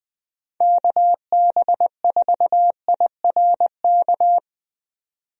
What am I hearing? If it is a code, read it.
Morse code, heard as a keyed tone, about 20 wpm, 700 Hz. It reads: KB4IRK
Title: USA Call Signs - 6 Characters - 20wpm